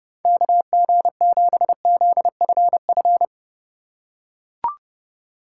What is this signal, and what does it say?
Morse code, heard as a keyed tone, about 30 wpm, 700 Hz. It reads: KG7ZFF